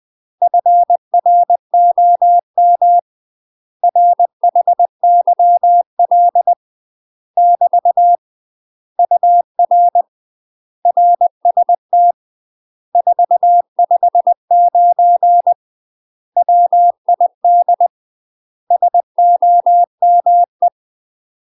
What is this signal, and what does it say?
Morse code, heard as a keyed tone, about 20 wpm, 700 Hz. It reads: FROM RHYL <BT> UR RST 459 WID SOME
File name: UK MASTER QSOs TRAINING / 20WPM / UK MASTER QSOs 0045